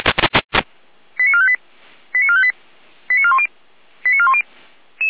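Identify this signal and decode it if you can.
Five-tone selective call (ZVEI), heard as quick runs of five tones, tones of about 70 ms. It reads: 89458 89458 89410 89410
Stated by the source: ZVEI five-tone selective call received on an SDR.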